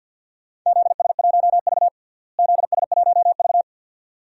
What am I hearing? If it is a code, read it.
Morse code, heard as a keyed tone, about 49 wpm, 700 Hz. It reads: ZS1V ZS1V
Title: DXCC-100 Call Signs - 4 Characters - Repeated 2x - 50wpm